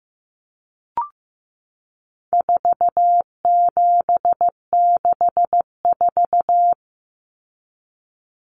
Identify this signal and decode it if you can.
Morse code, heard as a keyed tone, about 15 wpm, 700 Hz. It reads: 4764
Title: Numbers - 4-Digit 15wpm